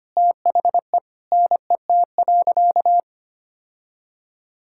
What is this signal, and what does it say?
Morse code, heard as a keyed tone, about 25 wpm, 700 Hz. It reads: THE NET.
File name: practice / 25wpm / Sentences with 9-Word Phrases - 25wpm